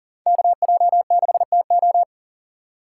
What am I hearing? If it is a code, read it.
Morse code, heard as a keyed tone, about 37 wpm, 700 Hz. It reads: KJ6TO
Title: USA Call Signs - 5 Characters - 2x Character Spacing - 40wpm